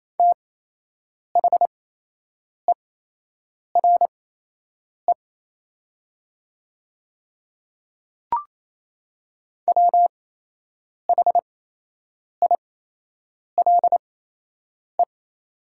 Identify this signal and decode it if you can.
Morse code, heard as a keyed tone, about 28 wpm, spaced as for 8 wpm, 700 Hz. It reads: THERE WHILE
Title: 8x Character Spacing - Top 500 Words - 28wpm